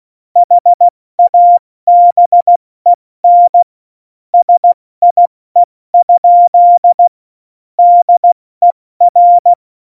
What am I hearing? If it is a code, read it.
Morse code, heard as a keyed tone, about 16 wpm, 700 Hz. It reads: HABEN SIE? DER